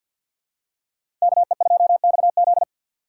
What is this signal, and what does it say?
Morse code, heard as a keyed tone, about 50 wpm, 700 Hz. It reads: KE2XZ
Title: USA Call Signs - 5 Characters - 4x Character Spacing - 50wpm